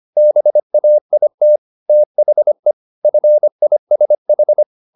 Morse code, heard as a keyed tone, about 25 wpm, 600 Hz. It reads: BAIT THE FISH
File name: the old man and the sea part 1 25wpm